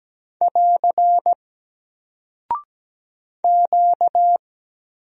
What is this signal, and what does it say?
Morse code, heard as a keyed tone, about 17 wpm, 700 Hz. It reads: <AR> Q
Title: Single Character (Letter:Number:Punctuation:Prosign) Alt 17wpm